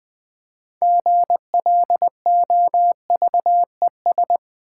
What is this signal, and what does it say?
Morse code, heard as a keyed tone, about 20 wpm, 700 Hz. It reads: GLOVES
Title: Easy Words with Hints - 20wpm